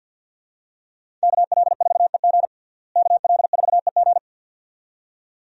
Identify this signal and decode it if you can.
Morse code, heard as a keyed tone, about 52 wpm, 700 Hz. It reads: KL4EG KL4EG
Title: USA Call Signs - 5 Characters - Repeated 2x - 50wpm